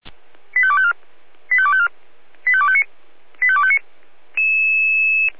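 Five-tone selective call (ZVEI), heard as quick runs of five tones, tones of about 70 ms. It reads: 85254 85254 85259 85259
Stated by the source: ZVEI five-tone selective call received on SDR.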